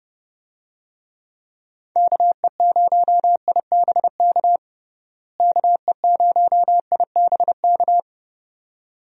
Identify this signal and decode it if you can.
Morse code, heard as a keyed tone, about 31 wpm, 700 Hz. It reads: KE0IBK KE0IBK
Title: USA Call Signs - 6 Characters - Repeated 2x - 30wpm